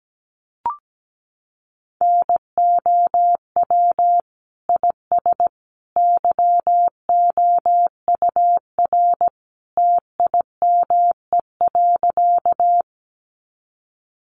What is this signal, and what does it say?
Morse code, heard as a keyed tone, about 17 wpm, 700 Hz. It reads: NOW IS YOUR TIME.
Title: Sentences from Top 100 Words 17wpm